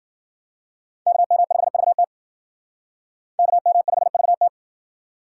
Morse code, heard as a keyed tone, about 57 wpm, 700 Hz. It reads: KM5VT KM5VT